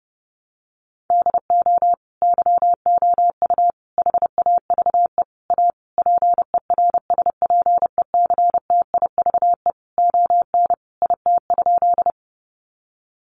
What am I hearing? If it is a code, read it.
Morse code, heard as a keyed tone, about 30 wpm, 700 Hz. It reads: DO YOU HAVE A PERSPECTIVE ON IT?
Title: Sentences from Top 1200 Words 30wpm